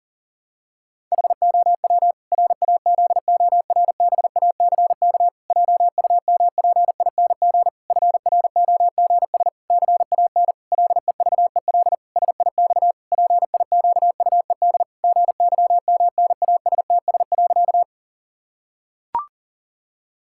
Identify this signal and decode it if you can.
Morse code, heard as a keyed tone, about 40 wpm, 700 Hz. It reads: HOW RAZORBACK JUMPING FROGS CAN LEVEL SIX PIQUED GYMNASTS.
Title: Pangram Challenge v2 - 40wpm